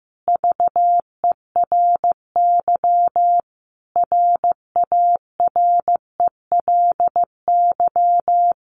Morse code, heard as a keyed tone, about 15 wpm, 700 Hz. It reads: VERY RARELY
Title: Sentences from Top 1400 Words 15wpm